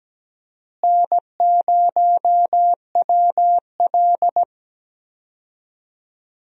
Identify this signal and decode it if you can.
Morse code, heard as a keyed tone, about 17 wpm, 700 Hz. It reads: N0WL
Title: USA Call Signs - 4 Characters - Repeated 2x - 17wpm